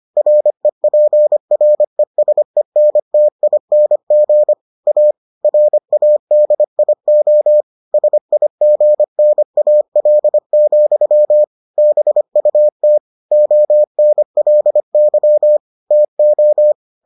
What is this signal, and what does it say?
Morse code, heard as a keyed tone, about 25 wpm, 600 Hz. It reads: REPRESENTING A RADIO SIGNAL, BUT ONLY TO